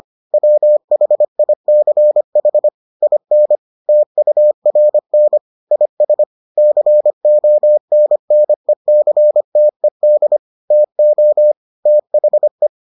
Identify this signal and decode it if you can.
Morse code, heard as a keyed tone, about 25 wpm, 600 Hz. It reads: WHICH IN TURN IS CONNECTED TO THE